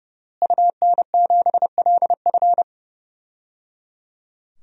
Morse code, heard as a keyed tone, about 30 wpm, 700 Hz. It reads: UN7LF
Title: DXCC-100 Call Signs - 5 Characters - Repeated 3x - 30wpm